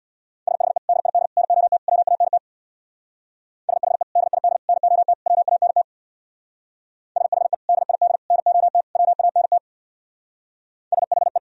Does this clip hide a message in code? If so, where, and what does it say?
Yes: Morse code from 0.5 to 11.4 s, heard as a keyed tone, about 65 wpm, 690 Hz. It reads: SHE DID NOT WANT SHE DID NOT WANT SHE DID NOT WANT SHE